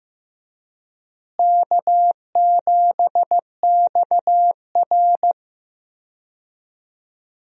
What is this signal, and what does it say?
Morse code, heard as a keyed tone, about 16 wpm, 700 Hz. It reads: K7XR
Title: USA Call Signs - 4 Characters - Repeated 2x - 15wpm